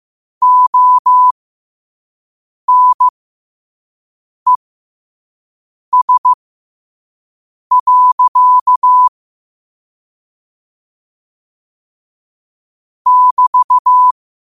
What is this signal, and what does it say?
Morse code, heard as a keyed tone, about 15 wpm, spaced as for 5 wpm, 1000 Hz. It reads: ONES. <BT>
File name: qotdfiles / 5wpm